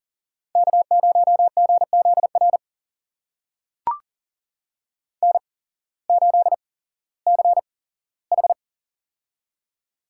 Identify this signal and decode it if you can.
Morse code, heard as a keyed tone, about 37 wpm, spaced as for 18 wpm, 700 Hz. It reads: K0GZR N8CH
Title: USA Call Signs - 8x Character Spacing - 40wpm